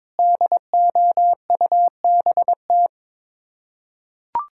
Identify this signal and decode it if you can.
Morse code, heard as a keyed tone, about 22 wpm, 700 Hz. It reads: DOUBT
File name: practice / 22wpm / Sets of 8 Words - 22wpm